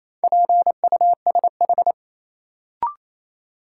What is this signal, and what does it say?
Morse code, heard as a keyed tone, about 27 wpm, 700 Hz. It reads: PUSH
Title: Sentences from Top 1300 Words - New Word Drill - 28wpm